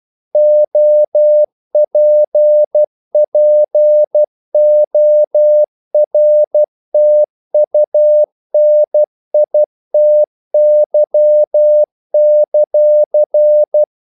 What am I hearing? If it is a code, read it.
Morse code, heard as a keyed tone, about 12 wpm, 600 Hz. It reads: OPPORTUNITY;
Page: English quotations in CW (telegraphy - morse)